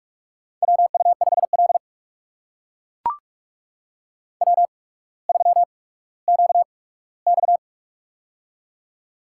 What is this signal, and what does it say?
Morse code, heard as a keyed tone, about 44 wpm, spaced as for 19 wpm, 700 Hz. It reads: WU5L W3QX
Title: USA Call Signs - 4 Characters - 8x Character Spacing - 45wpm